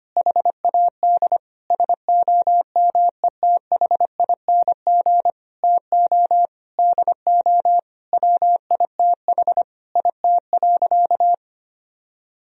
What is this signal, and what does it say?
Morse code, heard as a keyed tone, about 25 wpm, 700 Hz. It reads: HAD SOMETHING TO DO WITH IT.
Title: Sentences from Top 1000 Words - Review Part 2 - 25wpm